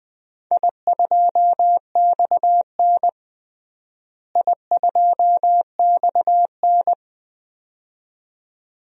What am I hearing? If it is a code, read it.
Morse code, heard as a keyed tone, about 21 wpm, 700 Hz. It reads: I2XN I2XN
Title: DXCC-100 Call Signs - 4 Characters - Repeated 2x - 20wpm